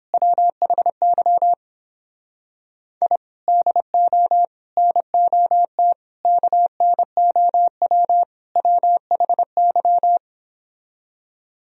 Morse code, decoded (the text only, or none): WHY I DO NOT KNOW WHY